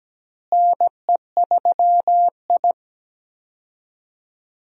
Morse code, heard as a keyed tone, about 17 wpm, 700 Hz. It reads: NE3I
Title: USA Call Signs - 4 Characters - Repeated 3x - 17wpm